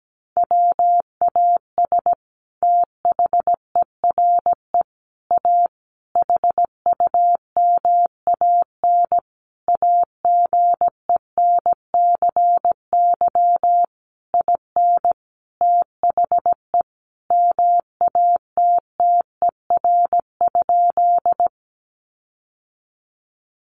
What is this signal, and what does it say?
Morse code, heard as a keyed tone, about 17 wpm, 700 Hz. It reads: WAS THERE A HUMAN AGENCY IN THE MATTER?
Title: Sentences from Top 1300 Words 17wpm